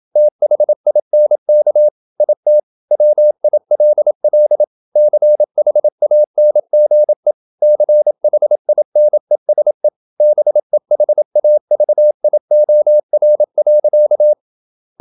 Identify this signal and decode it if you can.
Morse code, heard as a keyed tone, about 27 wpm, 600 Hz. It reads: THINK IT WILL CHANGE CHINESE BEHAVIOR.
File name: Never 0027wpm